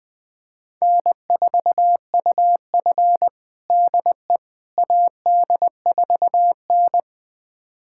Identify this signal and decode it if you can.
Morse code, heard as a keyed tone, about 20 wpm, 700 Hz. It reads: N4UF DE AD4N
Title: USA Callsign DE Callsign - 4 Characters - 20wpm